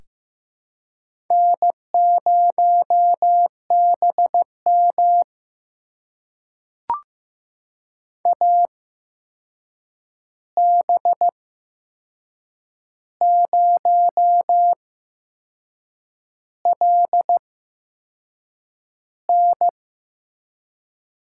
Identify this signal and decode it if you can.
Morse code, heard as a keyed tone, about 15 wpm, spaced as for 6 wpm, 700 Hz. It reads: N0BM AB0LN
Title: USA Call Signs - 8x Character Spacing - 15wpm